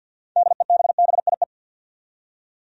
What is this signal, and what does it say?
Morse code, heard as a keyed tone, about 53 wpm, 700 Hz. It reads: DEBBIE
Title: USA Common Names - 50wpm